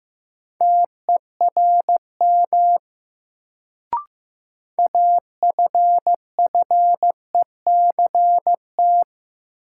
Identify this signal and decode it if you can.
Morse code, heard as a keyed tone, about 15 wpm, 700 Hz. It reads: TERM AFFECT
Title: Sentences from Top 900 Words - New Word Drill - 15wpm